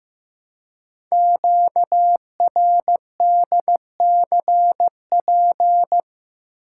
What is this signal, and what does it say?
Morse code, heard as a keyed tone, about 15 wpm, 700 Hz. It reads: QRDCP
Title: Code Groups - 5 Letter:Numbers - 15wpm